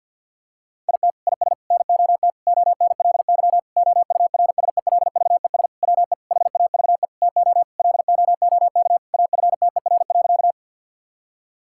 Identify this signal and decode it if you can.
Morse code, heard as a keyed tone, about 50 wpm, 700 Hz. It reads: IT IS NOT ONLY OURSELVES WE HAVE TO LOOK AFTER.
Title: Sentences from Top 1600 Words 50wpm